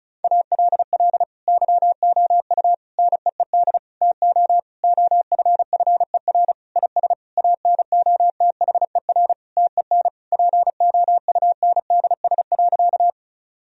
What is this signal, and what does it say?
Morse code, heard as a keyed tone, about 36 wpm, 700 Hz. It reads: ALL YOU NEED TO OFFER IS ANOTHER TEN POUNDS.
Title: Sentences from Top 1000 Words - Review Part 4 - 35wpm